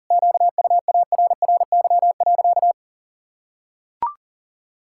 Morse code, heard as a keyed tone, about 40 wpm, 700 Hz. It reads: QUARRY.
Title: Pangram Challenge v2 - 40wpm